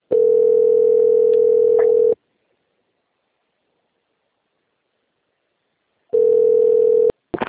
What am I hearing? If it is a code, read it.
Ringback tone (North America), 440 + 480 Hz.